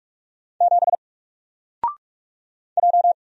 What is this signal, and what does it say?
Morse code, heard as a keyed tone, about 44 wpm, 700 Hz. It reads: 7 J
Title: Single Letter-Number Alt 45wpm